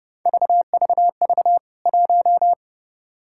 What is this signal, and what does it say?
Morse code, heard as a keyed tone, about 30 wpm, 700 Hz. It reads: VVV 1
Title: K1USN SST QSO Practice 30wpm